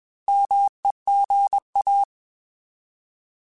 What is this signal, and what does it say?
Morse code, heard as a keyed tone, about 21 wpm, 800 Hz. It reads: MEGA
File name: Alphabetical / Alphabetical mp3s / Ms @20